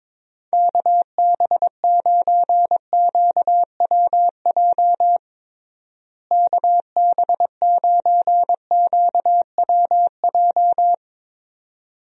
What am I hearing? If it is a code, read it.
Morse code, heard as a keyed tone, about 22 wpm, 700 Hz. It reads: KB9QWJ KB9QWJ
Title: USA Call Signs - 6 Characters - Repeated 2x - 22wpm